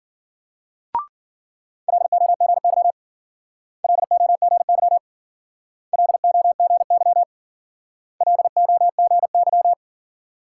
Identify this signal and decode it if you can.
Morse code, heard as a keyed tone, about 54 wpm, 700 Hz. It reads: LOGY LOGY LOGY LOGY